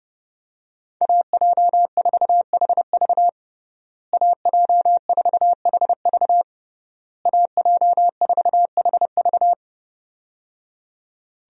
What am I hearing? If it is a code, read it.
Morse code, heard as a keyed tone, about 30 wpm, 700 Hz. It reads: AJ4HV AJ4HV AJ4HV
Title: USA Call Signs - 5 Characters - Repeated 3x - 30wpm